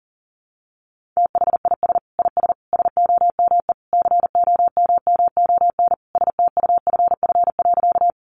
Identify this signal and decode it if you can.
Morse code, heard as a keyed tone, about 40 wpm, 700 Hz. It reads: THIS IS SOME COMMON STUFF.
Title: Sentences from Top 1400 Words 40wpm